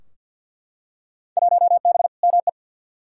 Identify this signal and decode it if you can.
Morse code, heard as a keyed tone, about 52 wpm, 700 Hz. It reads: 1B ME
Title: Field Day Exchanges - 50wpm